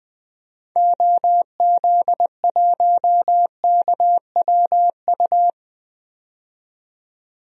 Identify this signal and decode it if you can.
Morse code, heard as a keyed tone, about 20 wpm, 700 Hz. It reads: OZ1KWU
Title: DXCC-100 Call Signs - 6 Characters - 20wpm